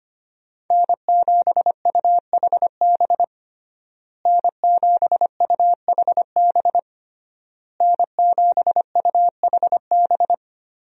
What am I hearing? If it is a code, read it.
Morse code, heard as a keyed tone, about 25 wpm, 700 Hz. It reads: N7UHB N7UHB N7UHB
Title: USA Call Signs - 5 Characters - Repeated 3x - 25wpm